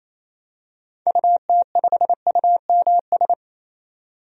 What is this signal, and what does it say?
Morse code, heard as a keyed tone, about 28 wpm, 700 Hz. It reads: UT5UMS
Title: DXCC-100 Call Signs - 6 Characters - 28wpm